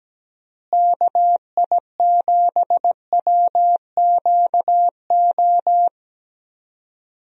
Morse code, heard as a keyed tone, about 17 wpm, 700 Hz. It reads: KI7WQO
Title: USA Call Signs - 6 Characters - Repeated 3x - 17wpm